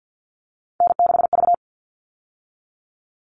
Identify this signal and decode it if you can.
Morse code, heard as a keyed tone, about 50 wpm, 700 Hz. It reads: N6V
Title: All Cut Numbers - 3-Digit 50wpm